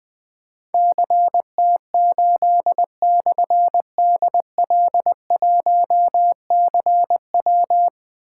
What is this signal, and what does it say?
Morse code, heard as a keyed tone, about 20 wpm, 700 Hz. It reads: CT8/DL1CW
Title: Difficult Call Signs 20wpm